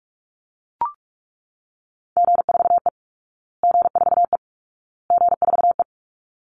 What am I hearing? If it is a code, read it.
Morse code, heard as a keyed tone, about 44 wpm, 700 Hz. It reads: G4E G4E G4E
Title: Call Signs Repeated 2x - 45wpm